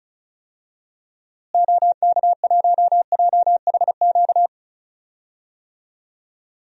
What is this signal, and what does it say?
Morse code, heard as a keyed tone, about 34 wpm, 700 Hz. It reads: OK1JHQ
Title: DXCC-100 Call Signs - 6 Characters - Repeated 2x - 35wpm